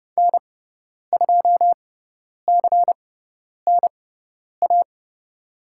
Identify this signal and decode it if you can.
Morse code, heard as a keyed tone, about 30 wpm, spaced as for 10 wpm, 700 Hz. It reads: N2CNA
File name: Lesson 18g - Letter m - Call Signs 30wpm